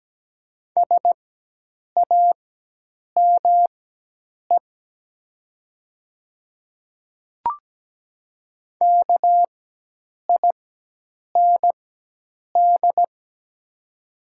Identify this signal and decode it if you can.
Morse code, heard as a keyed tone, about 17 wpm, spaced as for 8 wpm, 700 Hz. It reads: SAME KIND
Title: 4x Character Spacing - Top 300 Words - 17wpm